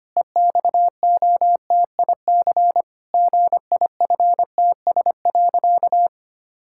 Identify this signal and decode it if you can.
Morse code, heard as a keyed tone, about 25 wpm, 700 Hz. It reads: EXOTIC GIFTS.
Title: Pangram Challenge v2 - 25wpm